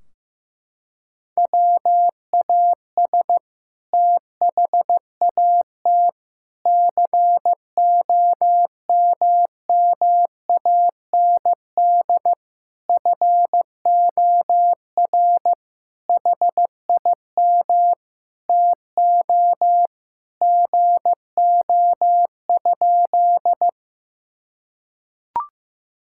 Morse code, heard as a keyed tone, about 15 wpm, 700 Hz. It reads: WAS THAT COMMAND FOR HIM TO GO?
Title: Sentences from Top 1600 Words 15wpm